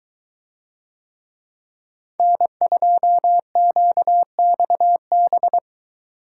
Morse code, heard as a keyed tone, about 23 wpm, 700 Hz. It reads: N2QXB